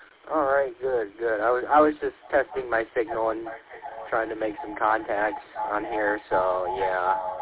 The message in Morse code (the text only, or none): none